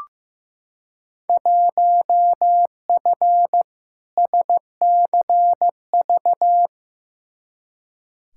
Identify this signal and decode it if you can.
Morse code, heard as a keyed tone, about 15 wpm, 700 Hz. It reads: 1F SCV
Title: Field Day Exchanges Encore - 15wpm